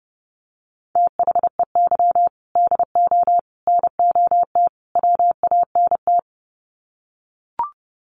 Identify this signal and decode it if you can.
Morse code, heard as a keyed tone, about 30 wpm, 700 Hz. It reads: THEY DO NOT WANT
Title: Sets of 4 Words 30wpm